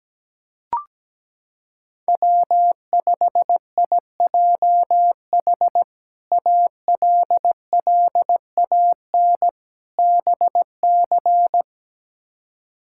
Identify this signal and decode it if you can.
Morse code, heard as a keyed tone, about 17 wpm, 700 Hz. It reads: W5IJH ALLAN BC
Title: K1USN SST QSO Practice - 17wpm